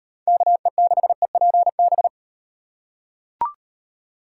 Morse code, heard as a keyed tone, about 35 wpm, 700 Hz. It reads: KE6EPB